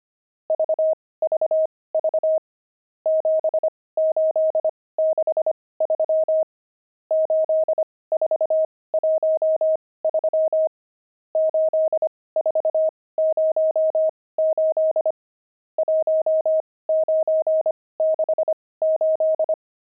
Morse code, heard as a keyed tone, about 25 wpm, spaced as for 18 wpm, 630 Hz. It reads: VVV 7863 8413 8408 1968
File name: pangram-fs-18-cs-25-nums1